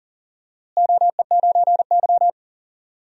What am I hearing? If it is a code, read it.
Morse code, heard as a keyed tone, about 40 wpm, 700 Hz. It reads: OE9Y
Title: DXCC-100 Call Signs - 4 Characters - Repeated 2x - 40wpm